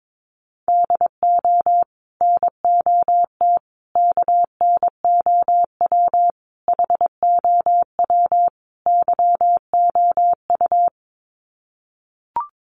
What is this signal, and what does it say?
Morse code, heard as a keyed tone, about 22 wpm, 700 Hz. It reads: DO NOT KNOW HOW YOU
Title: Sets of 5 Words Encore 22wpm